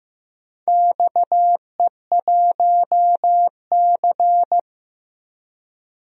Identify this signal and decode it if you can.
Morse code, heard as a keyed tone, about 15 wpm, 700 Hz. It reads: XE1C
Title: DXCC-100 Call Signs - 4 Characters - 15wpm